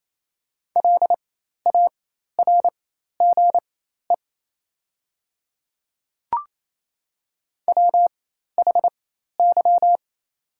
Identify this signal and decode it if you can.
Morse code, heard as a keyed tone, about 28 wpm, spaced as for 13 wpm, 690 Hz. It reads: LARGE WHY
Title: 4x Character Spacing - Top 300 Words - 28wpm